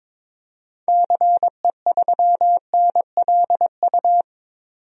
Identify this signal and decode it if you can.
Morse code, heard as a keyed tone, about 22 wpm, 700 Hz. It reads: CE3NLU
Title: DXCC-100 Call Signs - 6 Characters - 22wpm